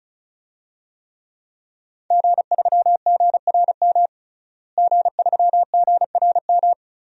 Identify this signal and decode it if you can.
Morse code, heard as a keyed tone, about 36 wpm, 700 Hz. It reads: G3GRM G3GRM
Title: DXCC-100 Call Signs - 5 Characters - Repeated 3x - 35wpm